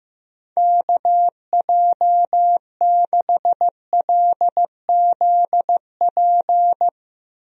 Morse code, heard as a keyed tone, about 15 wpm, 700 Hz. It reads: KJ6LZP
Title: USA Call Signs - 6 Characters - Repeated 2x - 15wpm